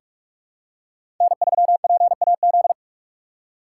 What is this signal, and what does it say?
Morse code, heard as a keyed tone, about 44 wpm, 700 Hz. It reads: N3PAZ
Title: USA Call Signs - 5 Characters - 6x Character Spacing - 45wpm